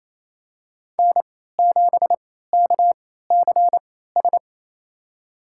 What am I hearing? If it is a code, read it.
Morse code, heard as a keyed tone, about 28 wpm, spaced as for 16 wpm, 700 Hz. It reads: N7KCS
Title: USA Call Signs - 3x Character Spacing - 28wpm